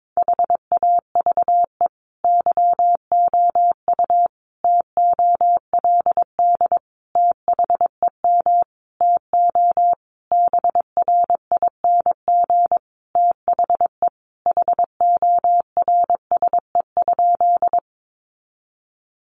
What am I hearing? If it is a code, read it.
Morse code, heard as a keyed tone, about 22 wpm, 700 Hz. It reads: HAVE YOU TOLD THEM TO BRING THE HORSE?
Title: Sentences from Top 500 Words 22wpm